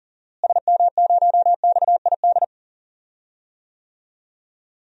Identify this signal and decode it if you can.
Morse code, heard as a keyed tone, about 37 wpm, 700 Hz. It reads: SM0XID
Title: DXCC-100 Call Signs - 6 Characters - Repeated 2x - 40wpm